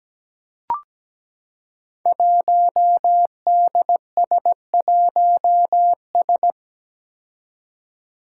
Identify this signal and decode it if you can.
Morse code, heard as a keyed tone, about 17 wpm, 700 Hz. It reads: 1DS1S
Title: Five Challenging Letter-Number Combinations 17wpm